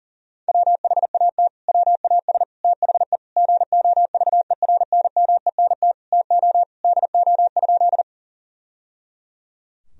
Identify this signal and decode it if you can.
Morse code, heard as a keyed tone, about 40 wpm, 700 Hz. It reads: WHAT WAS THE GOVERNMENT TO DO?
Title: Sentences from Top 1000 Words - Review Part 3 - 40wpm